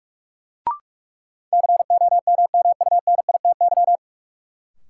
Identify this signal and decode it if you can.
Morse code, heard as a keyed tone, about 44 wpm, 700 Hz. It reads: COMMUNITY
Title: Sentences from Top 1000 Words - Review Part 4 - New Word Drill - 45wpm